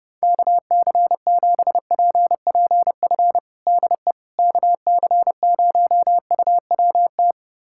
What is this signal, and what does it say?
Morse code, heard as a keyed tone, about 30 wpm, 700 Hz. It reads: KC7PPF DE KC0UWT
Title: USA Callsign DE Callsign - 6 Characters - 30wpm